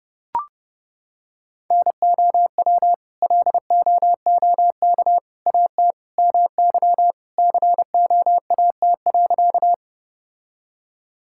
Morse code, heard as a keyed tone, about 30 wpm, 700 Hz. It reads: NOW LOOK AT MY COAT.
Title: Sentences from Top 1600 Words 30wpm